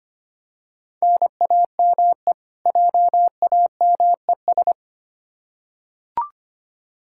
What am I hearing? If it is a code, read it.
Morse code, heard as a keyed tone, about 25 wpm, 700 Hz. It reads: NAME JAMES
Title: Practice QSO - Exchange 1 - Beginning - 25wpm